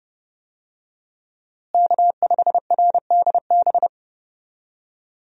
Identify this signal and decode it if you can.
Morse code, heard as a keyed tone, about 30 wpm, 700 Hz. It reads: K5RDB
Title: USA Call Signs - 5 Characters - Repeated 3x - 30wpm